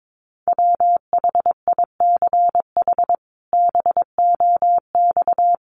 Morse code, heard as a keyed tone, about 22 wpm, 700 Hz. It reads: WHICH BOX
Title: Sentences from Top 500 Words 22wpm